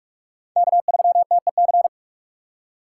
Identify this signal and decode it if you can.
Morse code, heard as a keyed tone, about 42 wpm, 700 Hz. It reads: K3TEC